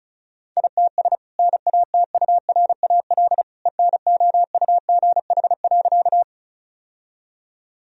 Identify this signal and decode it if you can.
Morse code, heard as a keyed tone, about 36 wpm, 700 Hz. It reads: ITS NATURAL ENOUGH.
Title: Sentences from Top 1000 Words - Review Part 4 - 35wpm